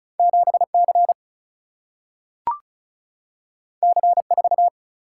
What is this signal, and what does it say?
Morse code, heard as a keyed tone, about 36 wpm, 700 Hz. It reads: 7 C C 4
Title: Single-Letter Single-Number 35wpm